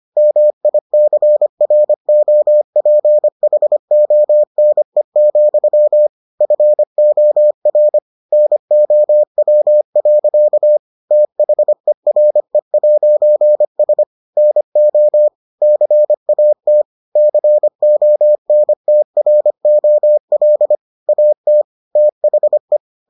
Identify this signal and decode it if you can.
Morse code, heard as a keyed tone, about 25 wpm, 600 Hz. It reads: MICROPHONE, FOR NOW. THERE'S NO CAT CONTROL AT THE